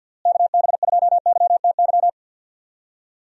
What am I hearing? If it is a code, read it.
Morse code, heard as a keyed tone, about 51 wpm, 700 Hz. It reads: KB2YTY
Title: USA Call Signs - 6 Characters - Repeated 3x - 50wpm